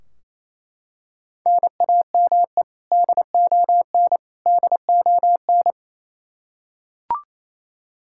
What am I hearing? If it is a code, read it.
Morse code, heard as a keyed tone, about 28 wpm, 700 Hz. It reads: NAME DON DON
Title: Practice QSO - Exchange 1 - Beginning - 28wpm